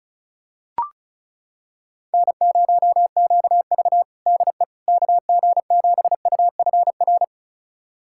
Morse code, heard as a keyed tone, about 36 wpm, 700 Hz. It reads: N0QV DE KG7UFR
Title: USA Callsign DE Callsign 35wpm